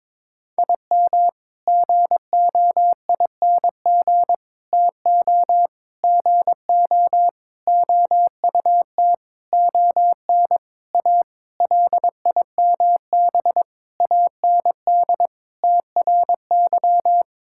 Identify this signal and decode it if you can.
Morse code, heard as a keyed tone, about 22 wpm, 700 Hz. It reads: IM GOING TO GO OUT ON A LIMB AND TRY
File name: practice / 22wpm / Sentences with 8-Word Phrases - 22wpm